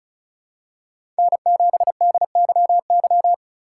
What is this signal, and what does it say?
Morse code, heard as a keyed tone, about 36 wpm, 700 Hz. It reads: N7DYY
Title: USA Call Signs - 5 Characters - 4x Character Spacing - 35wpm